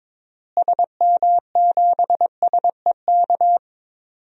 Morse code, heard as a keyed tone, about 22 wpm, 700 Hz. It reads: SM7SEK
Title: DXCC-100 Call Signs - 6 Characters - Repeated 3x - 22wpm